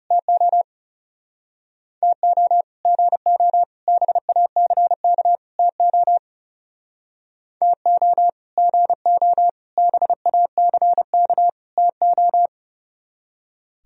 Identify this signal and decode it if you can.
Morse code, heard as a keyed tone, about 40 wpm, 700 Hz. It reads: TO TO GO BACK TO TO GO BACK TO